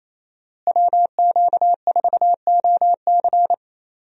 Morse code, heard as a keyed tone, about 28 wpm, 700 Hz. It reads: WQ4OC
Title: USA Call Signs - 5 Characters - Repeated 3x - 28wpm